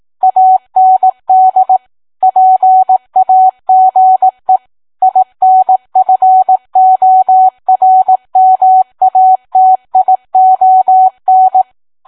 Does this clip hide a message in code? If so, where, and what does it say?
Yes: Morse code from 0.2 to 11.6 s, heard as a keyed tone, about 18 wpm, 750 Hz. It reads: AND PAGE INFORMATION